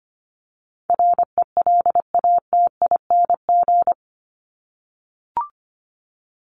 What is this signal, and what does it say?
Morse code, heard as a keyed tone, about 25 wpm, 700 Hz. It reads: RELATING
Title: ING Suffix - Top 100 Words with Base Words 25wpm